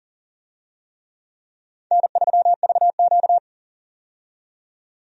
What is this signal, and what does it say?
Morse code, heard as a keyed tone, about 40 wpm, 700 Hz. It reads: N3VQ
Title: USA Call Signs - 4 Characters - 40wpm